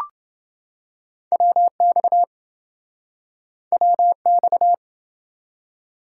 Morse code, heard as a keyed tone, about 26 wpm, 700 Hz. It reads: WX WX